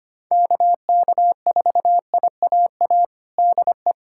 Morse code, heard as a keyed tone, about 25 wpm, 700 Hz. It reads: KK4IAA DE
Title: USA Callsign DE Callsign - 6 Characters - 25wpm